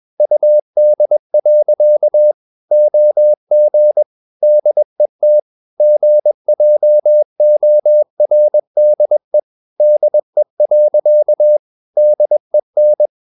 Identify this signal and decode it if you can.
Morse code, heard as a keyed tone, about 21 wpm, 600 Hz. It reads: UD. OG DET GJORDE DE. DEN
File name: Ferie QRS 21wpm